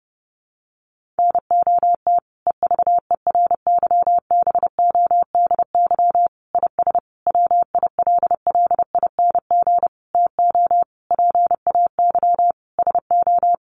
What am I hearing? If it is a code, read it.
Morse code, heard as a keyed tone, about 30 wpm, 700 Hz. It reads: NOT EVERYBODY IS WILLING TO PAY SO
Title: Sentences from Top 1400 Words 30wpm